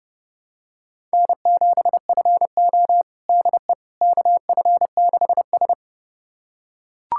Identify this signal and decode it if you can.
Morse code, heard as a keyed tone, about 32 wpm, 700 Hz. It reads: N7FO DE KF6S
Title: USA Callsign DE Callsign - 4 Characters - 30wpm